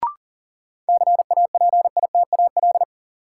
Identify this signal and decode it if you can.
Morse code, heard as a keyed tone, about 40 wpm, 700 Hz. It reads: CAPITAL
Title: Sentences from Top 1000 Words - New Word Drill - 40wpm